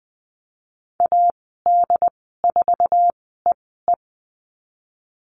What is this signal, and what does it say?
Morse code, heard as a keyed tone, about 20 wpm, spaced as for 15 wpm, 700 Hz. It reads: AD4EE
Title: 082 Lesson 14g - Number 2 - Call Signs 20wpm